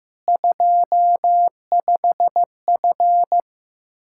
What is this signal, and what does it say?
Morse code, heard as a keyed tone, about 16 wpm, 700 Hz. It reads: 25F
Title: Practice QSO - Exchange 2 - Info - 15wpm